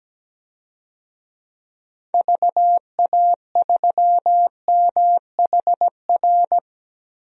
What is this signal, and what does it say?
Morse code, heard as a keyed tone, about 17 wpm, 700 Hz. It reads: VA3MHR
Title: DXCC-100 Call Signs - 6 Characters - 17wpm